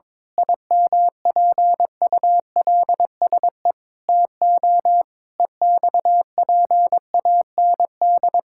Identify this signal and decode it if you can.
Morse code, heard as a keyed tone, about 22 wpm, 700 Hz. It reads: IMPULSE TO EXPAND
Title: Pangram Challenge v2 - 22wpm